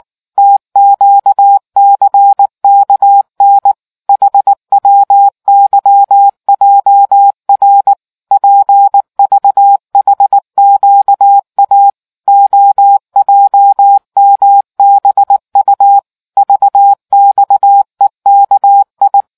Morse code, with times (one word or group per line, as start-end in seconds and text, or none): 0.4-3.7 TQCKN
4.1-7.9 HWYJR
8.3-11.9 PVHQA
12.3-16.0 OJMBU
16.4-19.2 VXEKI